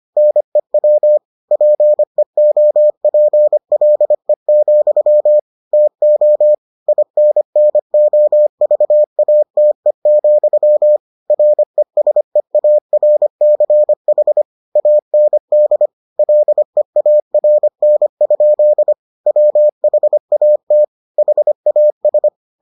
Morse code, heard as a keyed tone, about 25 wpm, 600 Hz. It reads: NEW PEOPLE, TO INNOVATE, RESEARCH AND LEARN? WHAT HAS